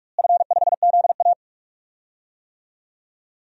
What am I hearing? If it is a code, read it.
Morse code, heard as a keyed tone, about 46 wpm, 700 Hz. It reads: F5ZA
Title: DXCC-100 Call Signs - 4 Characters - Repeated 3x - 45wpm